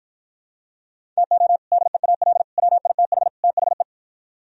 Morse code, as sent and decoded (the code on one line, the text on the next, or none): - --- / -.. .- .-.. / .-- .. - .... / - .... .
TO DEAL WITH THE